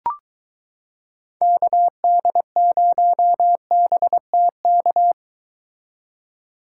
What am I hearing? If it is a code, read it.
Morse code, heard as a keyed tone, about 22 wpm, 700 Hz. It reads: KD0BTK